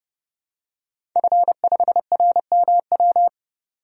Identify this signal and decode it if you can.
Morse code, heard as a keyed tone, about 30 wpm, 700 Hz. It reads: F5RMW
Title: DXCC-100 Call Signs - 5 Characters - 30wpm